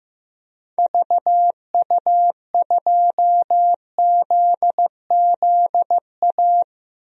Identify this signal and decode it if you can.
Morse code, heard as a keyed tone, about 15 wpm, 700 Hz. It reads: VU2ZZA
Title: DXCC-100 Call Signs - 6 Characters - Repeated 3x - 15wpm